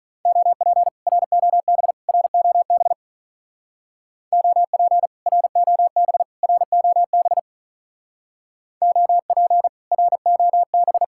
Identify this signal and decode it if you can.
Morse code, heard as a keyed tone, about 40 wpm, 700 Hz. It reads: OP ROB ROB OP ROB ROB OP ROB
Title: Speed-Racing Common QSO Phrases - 35wpm